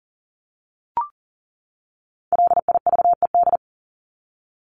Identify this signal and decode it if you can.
Morse code, heard as a keyed tone, about 40 wpm, 700 Hz. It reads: LIVED
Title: ED Suffix - Top 100 Words with Base Words 40wpm